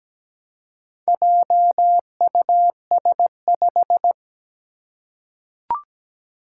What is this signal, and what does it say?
Morse code, heard as a keyed tone, about 17 wpm, 700 Hz. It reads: JUS5
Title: Four Challenging Letter-Number Combinations 17wpm